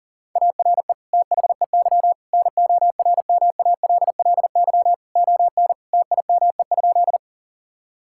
Morse code, heard as a keyed tone, about 37 wpm, 700 Hz. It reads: ARE THEY NORMALLY ON TIME?
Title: Sentences from Top 1000 Words - Review Part 6 - 40wpm